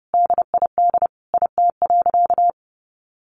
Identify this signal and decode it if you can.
Morse code, heard as a keyed tone, about 30 wpm, 700 Hz. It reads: DID IT.
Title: Sentences from Top 1500 Words 30wpm